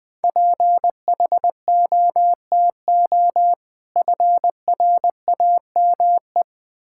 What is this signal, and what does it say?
Morse code, heard as a keyed tone, about 21 wpm, 700 Hz. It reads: PHOTO FRAME
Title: Difficult Words - Speed-Racing with Hints - 20wpm